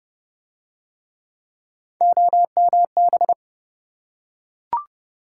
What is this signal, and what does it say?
Morse code, heard as a keyed tone, about 29 wpm, 700 Hz. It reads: OMB